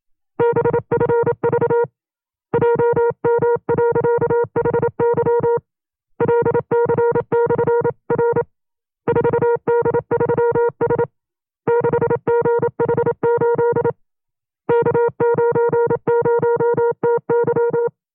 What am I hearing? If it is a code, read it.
Morse code, heard as a keyed tone, about 27 wpm, 470 Hz. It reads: BFV JM.HY LC/R 4D3S 6GH8 K90TY
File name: FreeFiles / CW-practice / 25WPM-side1